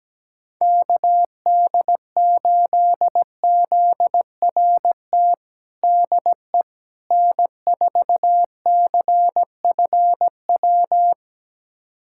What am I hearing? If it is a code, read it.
Morse code, heard as a keyed tone, about 17 wpm, 700 Hz. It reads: KD8ZRT DE N4CFW
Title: USA Callsign DE Callsign 17wpm